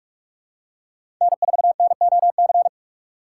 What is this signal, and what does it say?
Morse code, heard as a keyed tone, about 44 wpm, 700 Hz. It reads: N4NOC
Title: USA Call Signs - 5 Characters - 2x Character Spacing - 45wpm